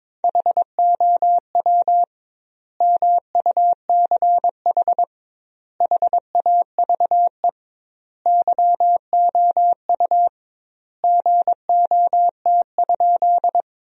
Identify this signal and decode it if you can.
Morse code, heard as a keyed tone, about 22 wpm, 700 Hz. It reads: HOW MUCH HAVE YOU GOT?
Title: 2x Word-Spacing - Sentences from Top 300 Words - 22wpm